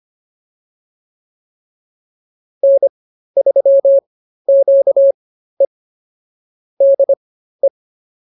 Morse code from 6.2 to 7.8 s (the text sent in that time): DE